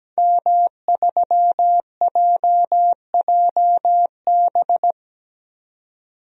Morse code, heard as a keyed tone, about 17 wpm, 700 Hz. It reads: M3JJB
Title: DXCC-100 Call Signs - 5 Characters - 17wpm